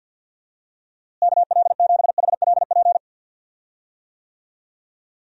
Morse code, heard as a keyed tone, about 51 wpm, 700 Hz. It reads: KL7HLP
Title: DXCC-100 Call Signs - 6 Characters - Repeated 3x - 50wpm